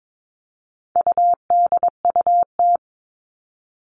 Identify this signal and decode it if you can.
Morse code, heard as a keyed tone, about 22 wpm, 700 Hz. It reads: UDUT
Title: All Cut Numbers - 4-Digit 22wpm